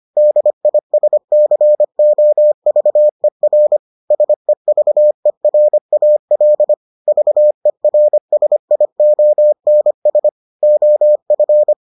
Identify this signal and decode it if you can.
Morse code, heard as a keyed tone, about 25 wpm, 600 Hz. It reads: DISCOVER SEVERAL VERSIONS OF